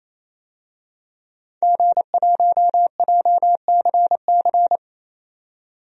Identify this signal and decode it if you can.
Morse code, heard as a keyed tone, about 28 wpm, 700 Hz. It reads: G1JCC
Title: DXCC-100 Call Signs - 5 Characters - 28wpm